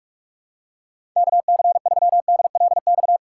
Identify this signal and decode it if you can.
Morse code, heard as a keyed tone, about 46 wpm, 700 Hz. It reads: KC3DLX
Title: USA Call Signs - 6 Characters - 2x Character Spacing - 45wpm